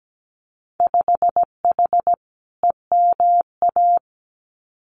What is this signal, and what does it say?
Morse code, heard as a keyed tone, about 17 wpm, 700 Hz. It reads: 5H EMA
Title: Winter Field Day Exchanges Encore 17wpm